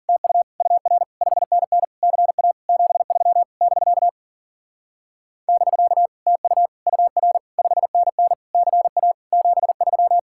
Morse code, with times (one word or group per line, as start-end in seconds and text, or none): 0.1-0.4 TU
0.6-1.0 UR
1.2-1.8 5NN
2.0-2.5 CA
2.7-3.4 73
3.6-4.1 <BK>
5.5-6.1 <BK>
6.3-6.7 TU
6.9-7.4 UR
7.6-8.3 5NN
8.5-9.1 CA
9.3-10.2 73